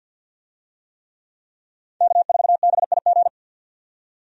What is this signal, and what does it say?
Morse code, heard as a keyed tone, about 50 wpm, 700 Hz. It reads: K4BIG
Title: USA Call Signs - 5 Characters - 50wpm